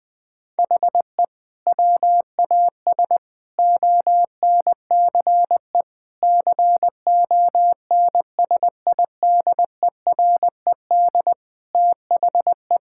Morse code, heard as a keyed tone, about 20 wpm, 700 Hz. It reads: HE WAS ONCE CONSIDERED THE
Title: Sentences with 7-Word Phrases - 20wpm